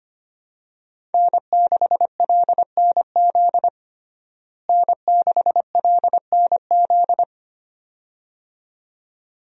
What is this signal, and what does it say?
Morse code, heard as a keyed tone, about 25 wpm, 700 Hz. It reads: N6LNZ N6LNZ
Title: USA Call Signs - 5 Characters - Repeated 2x - 25wpm